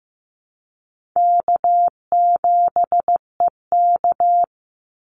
Morse code, heard as a keyed tone, about 15 wpm, 700 Hz. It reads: K7EK
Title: Easy Call Signs - 15wpm